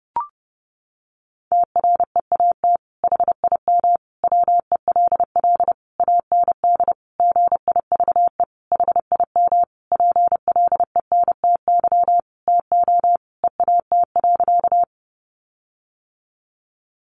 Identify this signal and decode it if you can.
Morse code, heard as a keyed tone, about 28 wpm, 700 Hz. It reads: TREAT HIM WELL AND GIVE HIM PLENTY TO EAT.
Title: Sentences from Top 1300 Words 30wpm